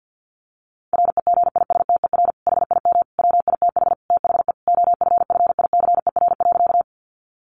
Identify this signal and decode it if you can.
Morse code, heard as a keyed tone, about 50 wpm, 700 Hz. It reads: REGISTER HIM WITH THE OFFICER.
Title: Sentences from Top 1200 Words 50wpm